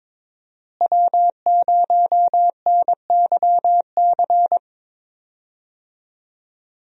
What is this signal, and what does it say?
Morse code, heard as a keyed tone, about 21 wpm, 700 Hz. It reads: W0NYC
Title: USA Call Signs - 5 Characters - Repeated 2x - 22wpm